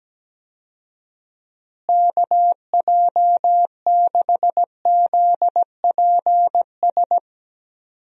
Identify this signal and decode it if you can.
Morse code, heard as a keyed tone, about 17 wpm, 700 Hz. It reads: KJ6ZPS
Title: USA Call Signs - 6 Characters - Repeated 2x - 17wpm